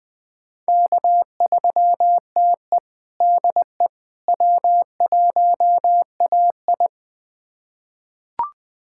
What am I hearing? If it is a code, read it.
Morse code, heard as a keyed tone, about 20 wpm, 700 Hz. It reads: K3TE DE W1AI
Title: USA Callsign DE Callsign - 4 Characters - 20wpm